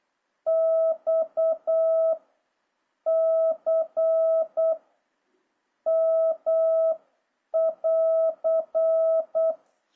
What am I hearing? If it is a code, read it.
Morse code, heard as a keyed tone, about 8 wpm, 650 Hz. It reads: X C M <AR>
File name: jh8cbh / 2026cw